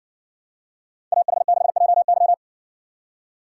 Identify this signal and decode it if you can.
Morse code, heard as a keyed tone, about 60 wpm, 700 Hz. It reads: AH6JY